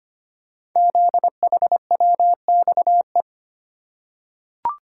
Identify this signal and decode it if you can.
Morse code, heard as a keyed tone, about 25 wpm, 700 Hz. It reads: ZHWXE
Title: Code Groups - 5 Letters - 25wpm